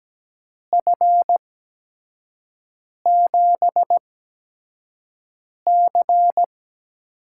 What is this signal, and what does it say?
Morse code, heard as a keyed tone, about 17 wpm, spaced as for 5 wpm, 700 Hz. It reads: F7C
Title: USA Call Signs - 5 Characters - 8x Character Spacing - 17wpm